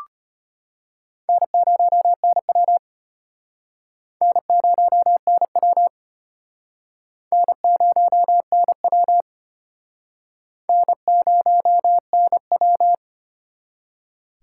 Morse code, heard as a keyed tone, about 34 wpm, 700 Hz. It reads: N0NW N0NW N0NW N0NW